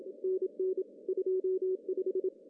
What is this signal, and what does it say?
Morse code, heard as a keyed tone, about 27 wpm, 370 Hz. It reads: NN 25
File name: CW / cqww-cw-k5tr